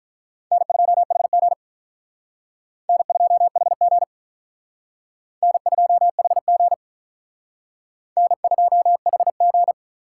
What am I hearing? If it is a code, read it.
Morse code, heard as a keyed tone, about 43 wpm, 700 Hz. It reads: N2HG N2HG N2HG N2HG
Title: Speed Racing - USA 1x2 Call Signs 35wpm